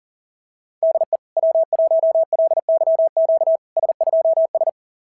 Morse code, heard as a keyed tone, about 40 wpm, 660 Hz. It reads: DE W1LYQ S2S
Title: SOTA QSO Practice 40wpm